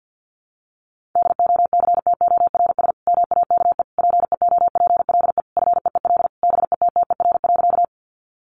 Morse code, heard as a keyed tone, about 50 wpm, 700 Hz. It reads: DOCTORS MAKE PEOPLE FEEL BETTER.
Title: Easy Phrases 50wpm